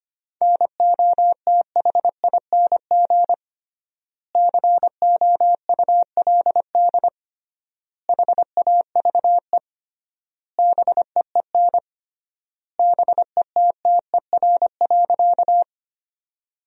Morse code, heard as a keyed tone, about 25 wpm, 700 Hz. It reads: NOTHING COULD HAVE BEEN BETTER.
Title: 3x Word-Spacing - Sentences from Top 500 Words - 25wpm